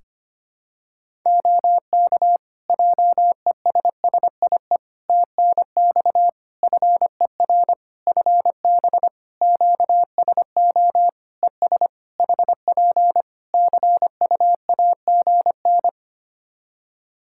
Morse code, heard as a keyed tone, about 25 wpm, 700 Hz. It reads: OK JESSIE TNX FER FB QSO ES HP CUAGN
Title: Practice QSO - Exchange 4 - Ending - 25wpm